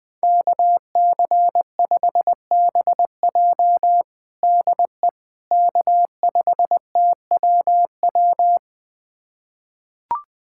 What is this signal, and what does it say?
Morse code, heard as a keyed tone, about 21 wpm, 700 Hz. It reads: KC5BJ DE K5TWW
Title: USA Callsign DE Callsign - 5 Characters - 20wpm